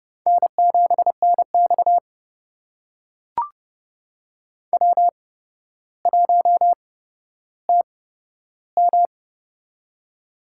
Morse code, heard as a keyed tone, about 30 wpm, spaced as for 13 wpm, 700 Hz. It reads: N7NX W1TM
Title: USA Call Signs - 4 Characters - 8x Character Spacing - 30wpm